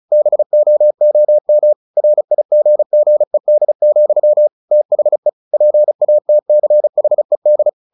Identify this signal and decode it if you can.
Morse code, heard as a keyed tone, about 34 wpm, 600 Hz. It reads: BOOM RIGGED, THE PATCHED
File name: the old man and the sea 2 35wpm